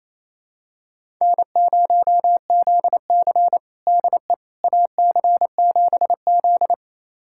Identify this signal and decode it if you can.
Morse code, heard as a keyed tone, about 28 wpm, 700 Hz. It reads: N0ZC DE AC7Z
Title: USA Callsign DE Callsign - 4 Characters - 28wpm